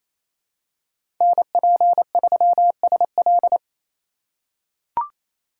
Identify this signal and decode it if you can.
Morse code, heard as a keyed tone, about 27 wpm, 700 Hz. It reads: NP3SL
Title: USA Call Signs - 5 Characters - 4x Character Spacing - 28wpm